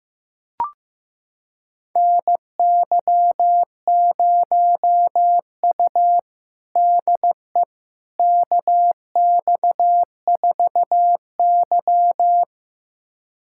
Morse code, heard as a keyed tone, about 15 wpm, 700 Hz. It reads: NY0U DE KX4Y
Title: USA Callsign DE Callsign - 4 Characters - 15wpm